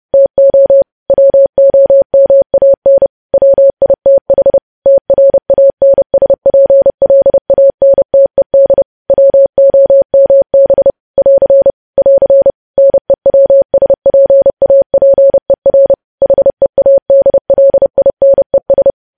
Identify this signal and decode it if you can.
Morse code, heard as a keyed tone, about 30 wpm, 550 Hz. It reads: TO WOMAN WITH TRANSPLANTED WOMB <AR> <AR> NEWSPAPER HEADLINES
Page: news headlines in morse code at 30 wpm